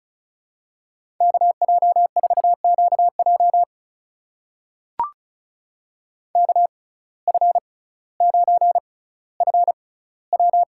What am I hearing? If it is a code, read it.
Morse code, heard as a keyed tone, about 36 wpm, spaced as for 18 wpm, 700 Hz. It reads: KJ4QJ KF9FW
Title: USA Call Signs - 5 Characters - 6x Character Spacing - 35wpm